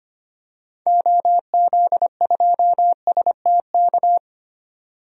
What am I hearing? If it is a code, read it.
Morse code, heard as a keyed tone, about 25 wpm, 700 Hz. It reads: OZ2STK
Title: DXCC-100 Call Signs - 6 Characters - 25wpm